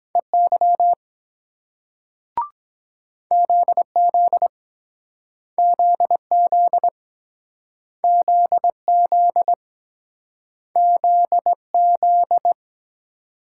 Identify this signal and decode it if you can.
Morse code, heard as a keyed tone, about 23 wpm, 700 Hz. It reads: EY ZZ ZZ ZZ ZZ